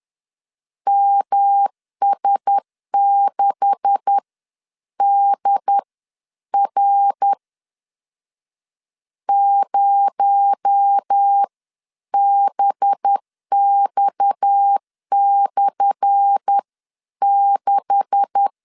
Morse code, heard as a keyed tone, about 10 wpm, spaced as for 9 wpm, 800 Hz. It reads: MS6DR 0BX/6
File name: LEZIONI / 10 SIGNS / 10C DOT 10wpm 2940